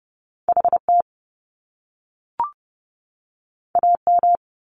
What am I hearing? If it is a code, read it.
Morse code, heard as a keyed tone, about 30 wpm, 700 Hz. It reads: HT AM
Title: Top Two Letter Combinations in Words 30wpm